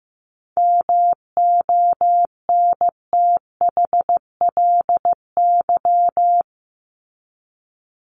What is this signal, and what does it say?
Morse code, heard as a keyed tone, about 15 wpm, 700 Hz. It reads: MONTHLY
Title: LY Suffix - Top 100 Words with Base Words 15wpm